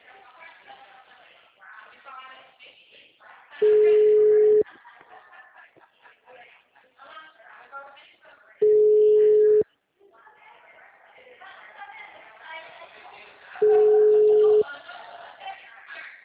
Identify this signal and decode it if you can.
Ringback tone (Europe, 425 Hz), with speech.